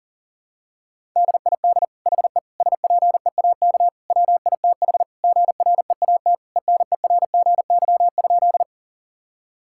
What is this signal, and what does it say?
Morse code, heard as a keyed tone, about 37 wpm, 700 Hz. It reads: DID HE SPEAK WITH GREAT ENERGY?
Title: Sentences from Top 1000 Words - Review Part 6 - 40wpm